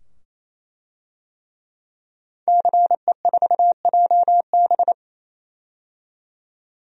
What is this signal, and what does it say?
Morse code, heard as a keyed tone, about 27 wpm, 700 Hz. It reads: CE4JB
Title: DXCC-100 Call Signs - 5 Characters - 28wpm